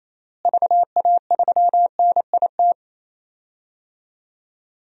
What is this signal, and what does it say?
Morse code, heard as a keyed tone, about 28 wpm, 700 Hz. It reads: VA3NIT
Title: DXCC-100 Call Signs - 6 Characters - Repeated 2x - 28wpm